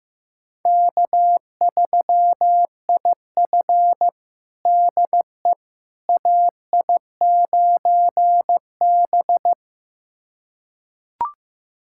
Morse code, heard as a keyed tone, about 16 wpm, 700 Hz. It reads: K3IF DE AI9B
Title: USA Callsign DE Callsign - 4 Characters - 15wpm